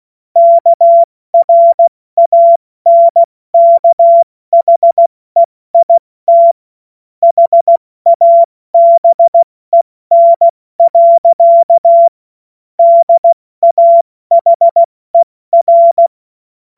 Morse code, heard as a keyed tone, about 16 wpm, spaced as for 14 wpm, 680 Hz. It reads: KRANKHEIT HABEN. DAHER
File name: Morsekurs / Witze / Witz 27. Juni 2023 70 F80